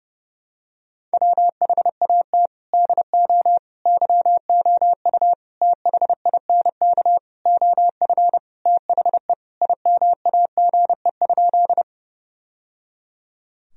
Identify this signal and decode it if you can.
Morse code, heard as a keyed tone, about 30 wpm, 700 Hz. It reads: WHAT DO YOU THINK OF THE IMAGE?
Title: Sentences from Top 1000 Words - Review Part 4 - 30wpm